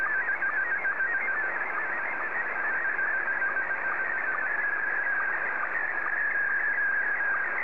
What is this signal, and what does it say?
SSTV (slow-scan television), Scottie 1.